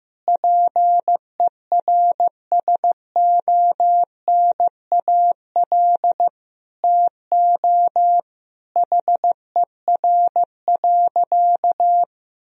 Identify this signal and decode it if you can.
Morse code, heard as a keyed tone, about 16 wpm, 700 Hz. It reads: PERSONAL TO HER.
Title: Sentences from Top 1000 Words - Review Part 4 - 15wpm